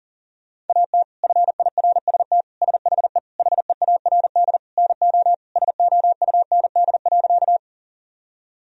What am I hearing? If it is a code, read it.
Morse code, heard as a keyed tone, about 40 wpm, 700 Hz. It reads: AT FIRST SHE HEARD NO SOUND.
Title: Sentences from Top 1000 Words - Review Part 1 - 40wpm